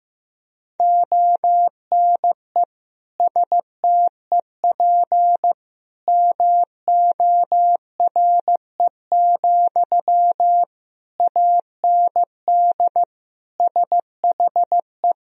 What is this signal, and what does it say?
Morse code, heard as a keyed tone, about 15 wpm, 700 Hz. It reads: ONE STEP MORE, AND SHE
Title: Sentences from Top 1000 Words - Review Part 3 - 15wpm